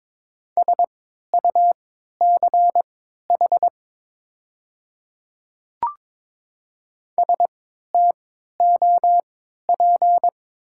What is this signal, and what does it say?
Morse code, heard as a keyed tone, about 22 wpm, spaced as for 13 wpm, 700 Hz. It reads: SUCH STOP
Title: 3x Character Spacing - Top 300 Words - 22wpm